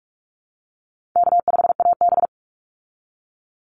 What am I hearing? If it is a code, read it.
Morse code, heard as a keyed tone, about 44 wpm, 700 Hz. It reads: K5AB
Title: Call Signs Repeated 3x - 45wpm